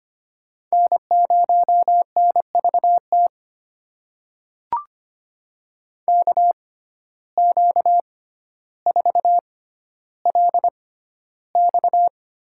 Morse code, heard as a keyed tone, about 25 wpm, spaced as for 13 wpm, 700 Hz. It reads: N0NVT KQ4LX
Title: USA Call Signs - 5 Characters - 6x Character Spacing - 25wpm